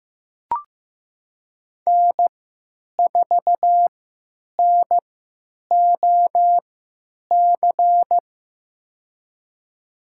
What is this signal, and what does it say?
Morse code, heard as a keyed tone, about 15 wpm, spaced as for 9 wpm, 700 Hz. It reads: N4NOC
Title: USA Call Signs - 5 Characters - 3x Character Spacing - 15wpm